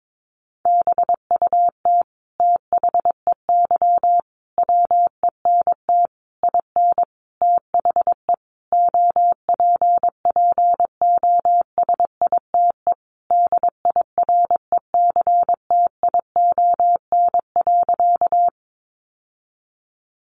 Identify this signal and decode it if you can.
Morse code, heard as a keyed tone, about 22 wpm, 700 Hz. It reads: BUT THEY WENT IN THE OPPOSITE DIRECTION.
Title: Sentences from Top 1300 Words 22wpm